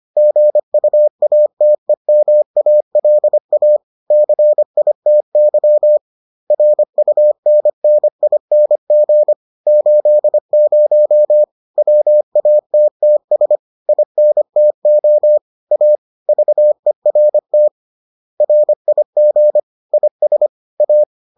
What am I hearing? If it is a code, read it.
Morse code, heard as a keyed tone, about 25 wpm, 600 Hz. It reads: GUATEMALA CITY RUNNING 80 WATTS INTO A VERT RIG IS A
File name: 25wpm-6